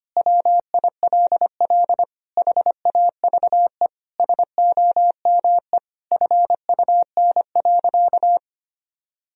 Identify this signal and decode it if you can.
Morse code, heard as a keyed tone, about 25 wpm, 690 Hz. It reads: WILL HAVE SOME FUN.
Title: Sentences from Top 1000 Words - Review Part 4 - 25wpm